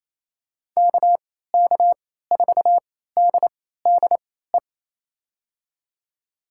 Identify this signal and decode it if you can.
Morse code, heard as a keyed tone, about 28 wpm, spaced as for 16 wpm, 700 Hz. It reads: KK4DDE
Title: USA Call Signs - 6 Characters - 3x Character Spacing - 28wpm